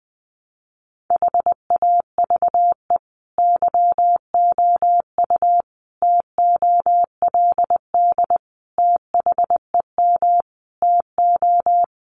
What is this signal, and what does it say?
Morse code, heard as a keyed tone, about 20 wpm, 700 Hz. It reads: HAVE YOU TOLD THEM TO
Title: Sentences from Top 500 Words 20wpm